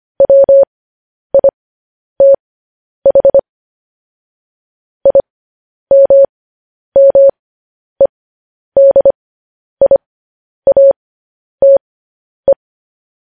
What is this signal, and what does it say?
Morse code, heard as a keyed tone, about 25 wpm, spaced as for 10 wpm, 550 Hz. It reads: WITH IMMEDIATE
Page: news-headlines-in-morse-code-at-10-wpm